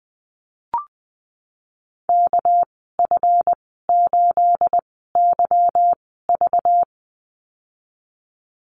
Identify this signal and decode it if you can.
Morse code, heard as a keyed tone, about 20 wpm, spaced as for 15 wpm, 700 Hz. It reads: KF8YV
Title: 219 Lesson 35g - Number 0 - Call Signs 20wpm